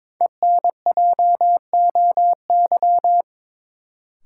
Morse code, heard as a keyed tone, about 22 wpm, 700 Hz. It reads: ENJOY
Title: Sentences from Top 1000 Words - Review Part 4 - New Word Drill - 22wpm